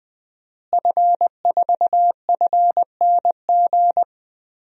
Morse code, heard as a keyed tone, about 20 wpm, 700 Hz. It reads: F4FNG
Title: DXCC-100 Call Signs - 5 Characters - Repeated 2x - 20wpm